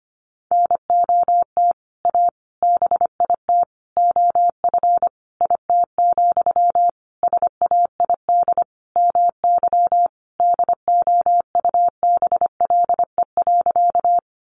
Morse code, heard as a keyed tone, about 25 wpm, 700 Hz. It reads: NOT A BIT OF IT, SAID MY DOUBLE.
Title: Sentences from Top 1400 Words 25wpm